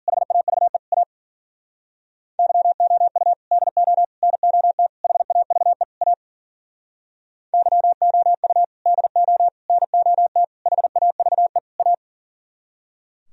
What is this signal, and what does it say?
Morse code, heard as a keyed tone, about 51 wpm, 700 Hz. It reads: HAVE A YOU DO NOT HAVE A YOU DO NOT HAVE A